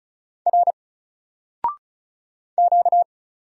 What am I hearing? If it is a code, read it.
Morse code, heard as a keyed tone, about 36 wpm, 700 Hz. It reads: R Q
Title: Single Character (Letter:Number:Punctuation:Prosign) Alt 35wpm